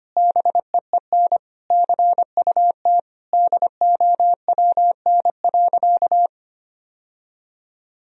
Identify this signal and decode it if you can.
Morse code, heard as a keyed tone, about 25 wpm, 700 Hz. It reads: BEEN CUT DOWN.
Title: Sentences from Top 1000 Words - Review Part 1 - 25wpm